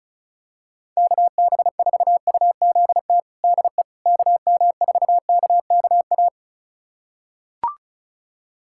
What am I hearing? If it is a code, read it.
Morse code, heard as a keyed tone, about 34 wpm, 700 Hz. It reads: KB4UZT DE KM4KKA
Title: USA Callsign DE Callsign - 6 Characters - 35wpm